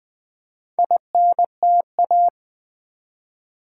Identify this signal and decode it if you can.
Morse code, heard as a keyed tone, about 20 wpm, 700 Hz. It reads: INTA
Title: Build A Word - 2 Letters At A Time - 6-Letter Words - 20wpm